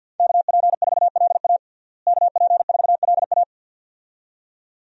Morse code, heard as a keyed tone, about 51 wpm, 700 Hz. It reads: KP4LA KP4LA
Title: USA Call Signs - 5 Characters - Repeated 2x - 50wpm